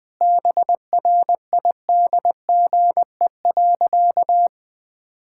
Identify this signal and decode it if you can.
Morse code, heard as a keyed tone, about 21 wpm, 700 Hz. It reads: BRIDGE.
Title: Sentences from Top 1600 Words 20wpm